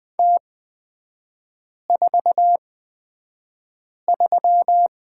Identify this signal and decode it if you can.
Morse code, heard as a keyed tone, about 20 wpm, spaced as for 5 wpm, 690 Hz. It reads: T43
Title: 236 Lesson 38e - Letter q - Three Characters 20wpm